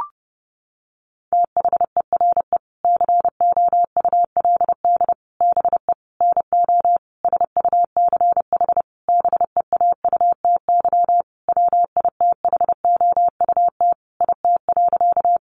Morse code, heard as a keyed tone, about 28 wpm, 700 Hz. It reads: THERE COULD BE NO SUCH BEAUTY WITHOUT IT.
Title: Sentences from Top 600 Words 30wpm